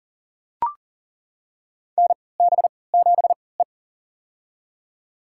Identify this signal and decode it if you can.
Morse code, heard as a keyed tone, about 42 wpm, spaced as for 23 wpm, 700 Hz. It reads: NB7E
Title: USA Call Signs - 4 Characters - 3x Character Spacing - 40wpm